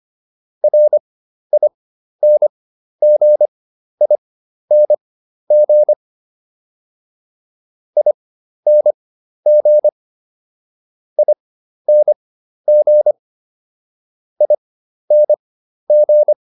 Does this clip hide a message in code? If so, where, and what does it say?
Yes: Morse code from 0.6 to 16.3 s, heard as a keyed tone, about 25 wpm, spaced as for 12 wpm, 600 Hz. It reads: RINGING ING ING ING